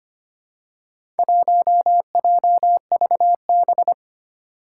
Morse code, heard as a keyed tone, about 25 wpm, 700 Hz. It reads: 1JVB
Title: Four Challenging Letter-Number Combinations 25wpm